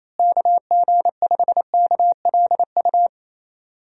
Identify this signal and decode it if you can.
Morse code, heard as a keyed tone, about 27 wpm, 700 Hz. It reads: KG5KLU
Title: USA Call Signs - 6 Characters - Repeated 2x - 28wpm